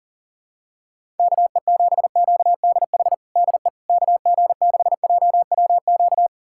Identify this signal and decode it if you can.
Morse code, heard as a keyed tone, about 40 wpm, 700 Hz. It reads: KE7QDH DE KG6JWQ
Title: USA Callsign DE Callsign - 6 Characters - 40wpm